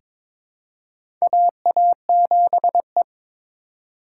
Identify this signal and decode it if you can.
Morse code, heard as a keyed tone, about 22 wpm, 700 Hz. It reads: AA7E
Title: USA Call Signs - 2x Character Spacing - 22wpm